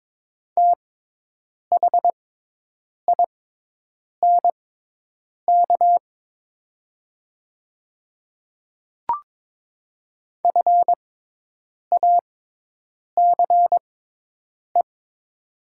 Morse code, heard as a keyed tone, about 23 wpm, spaced as for 8 wpm, 700 Hz. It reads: THINK FACE
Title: 6x Character Spacing - Top 500 Words - 22wpm